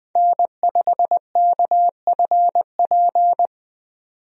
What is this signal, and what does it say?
Morse code, heard as a keyed tone, about 20 wpm, 700 Hz. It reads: N5KFP
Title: USA Call Signs - 5 Characters - 20wpm